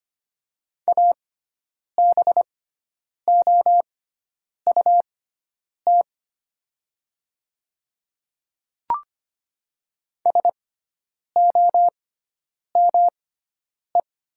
Morse code, heard as a keyed tone, about 25 wpm, spaced as for 9 wpm, 700 Hz. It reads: ABOUT SOME
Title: 6x Character Spacing - Top 100 Words - 25wpm